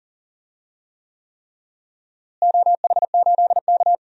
Morse code, heard as a keyed tone, about 40 wpm, 700 Hz. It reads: OH8K
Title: DXCC-100 Call Signs - 4 Characters - Repeated 3x - 40wpm